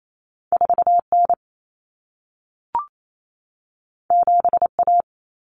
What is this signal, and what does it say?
Morse code, heard as a keyed tone, about 28 wpm, 700 Hz. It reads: 4N 7A
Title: Common Cut Numbers - 2-Digit 28wpm